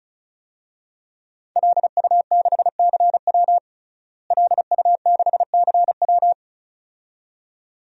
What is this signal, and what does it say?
Morse code, heard as a keyed tone, about 34 wpm, 700 Hz. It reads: LU6CW LU6CW
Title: DXCC-100 Call Signs - 5 Characters - Repeated 2x - 35wpm